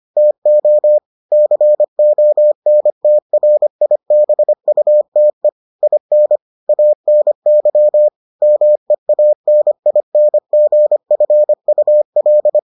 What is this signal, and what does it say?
Morse code, heard as a keyed tone, about 25 wpm, 600 Hz. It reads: TO CONTRIBUTE IN ANY MEANINGFUL